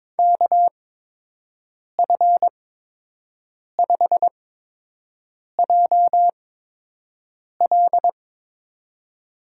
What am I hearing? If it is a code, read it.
Morse code, heard as a keyed tone, about 22 wpm, spaced as for 6 wpm, 700 Hz. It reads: KF5JL
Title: USA Call Signs - 5 Characters - 8x Character Spacing - 22wpm